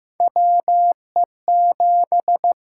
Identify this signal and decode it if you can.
Morse code, heard as a keyed tone, about 15 wpm, 700 Hz. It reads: WE7
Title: USA Call Signs - 4 Characters - 6x Character Spacing - 15wpm